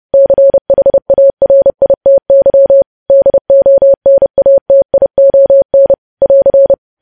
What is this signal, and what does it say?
Morse code, heard as a keyed tone, about 30 wpm, 550 Hz. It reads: CHARITY DONATION <AR>
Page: news headlines in morse code at 30 wpm